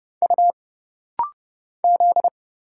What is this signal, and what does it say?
Morse code, heard as a keyed tone, about 29 wpm, 700 Hz. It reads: U Z
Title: Single Letters - Rapid-Fire 30wpm